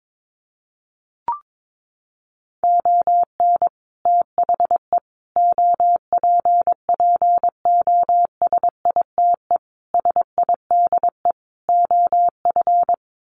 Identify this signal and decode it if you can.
Morse code, heard as a keyed tone, about 22 wpm, 700 Hz. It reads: ON THE OPPOSITE SIDE OF
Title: Sets of 5 Words Encore2 22wpm